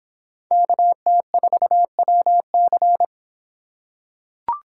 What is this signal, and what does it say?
Morse code, heard as a keyed tone, about 24 wpm, 700 Hz. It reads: KT4WC